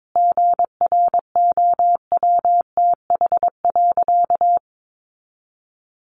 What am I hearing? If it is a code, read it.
Morse code, heard as a keyed tone, about 22 wpm, 700 Hz. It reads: GROWTH.
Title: Sentences from Top 1300 Words 22wpm